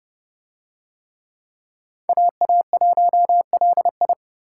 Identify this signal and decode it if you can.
Morse code, heard as a keyed tone, about 28 wpm, 700 Hz. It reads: AA1LI